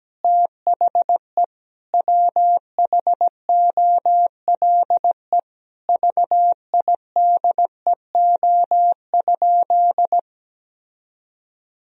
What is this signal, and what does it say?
Morse code, heard as a keyed tone, about 17 wpm, 700 Hz. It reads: THE WHOLE VIDEO?
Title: Sentences from Top 1000 Words - Review Part 4 - 17wpm